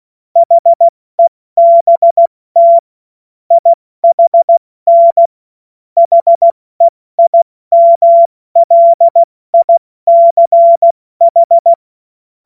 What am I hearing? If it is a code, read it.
Morse code, heard as a keyed tone, about 16 wpm, 680 Hz. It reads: HEBT IHN HEIMLICH
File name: Morsekurs / Witze / Witz 18. April 70 F80